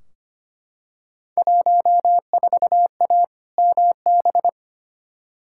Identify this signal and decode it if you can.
Morse code, heard as a keyed tone, about 25 wpm, 700 Hz. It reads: 14A MB
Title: Field Day Exchanges Encore - 25wpm